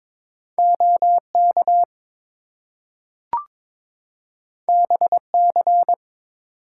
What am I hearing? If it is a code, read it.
Morse code, heard as a keyed tone, about 22 wpm, 700 Hz. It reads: OK BC
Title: US States and Canadian Provinces 22wpm